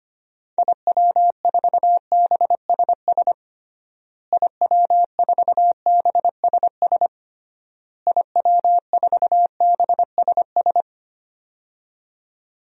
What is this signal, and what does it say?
Morse code, heard as a keyed tone, about 25 wpm, 700 Hz. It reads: IW4BSS IW4BSS IW4BSS
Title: DXCC-100 Call Signs - 6 Characters - Repeated 3x - 25wpm